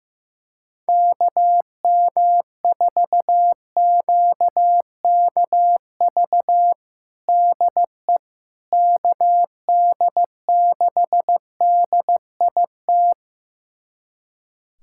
Morse code, heard as a keyed tone, about 16 wpm, 700 Hz. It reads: KM4QKV DE KD6DIT
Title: USA Callsign DE Callsign - 6 Characters - 15wpm